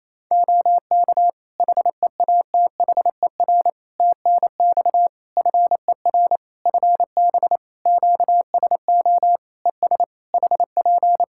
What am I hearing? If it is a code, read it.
Morse code, heard as a keyed tone, about 27 wpm, 700 Hz. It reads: OK HEATHER TNX FER FB QSO ES HP
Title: K1USN SST QSO Practice 28wpm